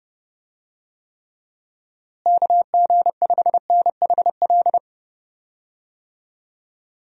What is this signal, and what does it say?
Morse code, heard as a keyed tone, about 31 wpm, 700 Hz. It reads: KG5NHL
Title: USA Call Signs - 6 Characters - Repeated 2x - 30wpm